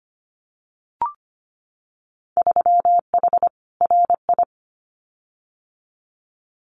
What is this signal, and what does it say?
Morse code, heard as a keyed tone, about 25 wpm, 700 Hz. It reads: 3H RI
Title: Winter Field Day Exchanges Encore 25wpm